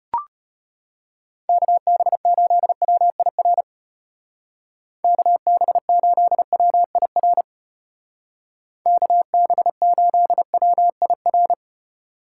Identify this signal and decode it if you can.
Morse code, heard as a keyed tone, about 33 wpm, 700 Hz. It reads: KB8WIR KB8WIR KB8WIR